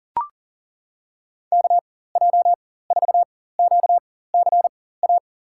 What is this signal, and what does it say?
Morse code, heard as a keyed tone, about 40 wpm, spaced as for 19 wpm, 700 Hz. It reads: KJ4QCA
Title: USA Call Signs - 6 Characters - 4x Character Spacing - 40wpm